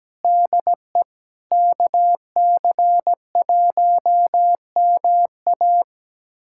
Morse code, heard as a keyed tone, about 17 wpm, 700 Hz. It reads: DE KC1MA
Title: USA Callsign DE Callsign - 5 Characters - 17wpm